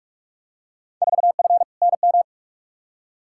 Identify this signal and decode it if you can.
Morse code, heard as a keyed tone, about 46 wpm, 700 Hz. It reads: 4F NM
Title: Field Day Exchanges Encore - 45wpm